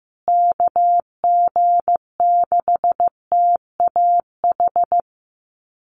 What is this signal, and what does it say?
Morse code, heard as a keyed tone, about 15 wpm, 700 Hz. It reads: KG6TAH
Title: USA Call Signs Repeated 3x 15wpm